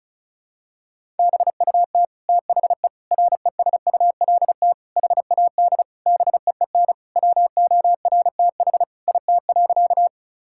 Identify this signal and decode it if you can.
Morse code, heard as a keyed tone, about 36 wpm, 700 Hz. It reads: BUT THE RESULT HAD BEEN WORTH IT.
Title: Sentences from Top 1000 Words - Review Part 5 - 35wpm